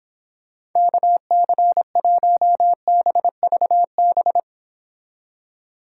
Morse code, heard as a keyed tone, about 26 wpm, 700 Hz. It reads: KC1BVB